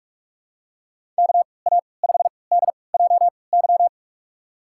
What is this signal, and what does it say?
Morse code, heard as a keyed tone, about 46 wpm, spaced as for 26 wpm, 700 Hz. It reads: KA5DJY
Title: USA Call Signs - 6 Characters - 3x Character Spacing - 45wpm